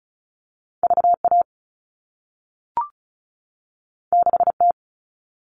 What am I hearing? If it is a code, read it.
Morse code, heard as a keyed tone, about 36 wpm, 690 Hz. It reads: VA 6T
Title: All Cut Numbers - 2-Digit 35wpm